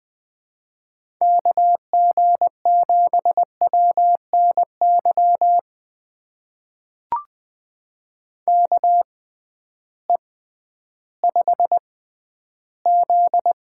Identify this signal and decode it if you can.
Morse code, heard as a keyed tone, about 20 wpm, spaced as for 12 wpm, 700 Hz. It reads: KG7WNY KE5Z
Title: USA Call Signs - 6 Characters - 6x Character Spacing - 20wpm